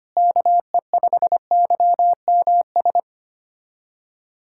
Morse code, heard as a keyed tone, about 25 wpm, 700 Hz. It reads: KE5YMS
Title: USA Callsign DE Callsign - 6 Characters - 25wpm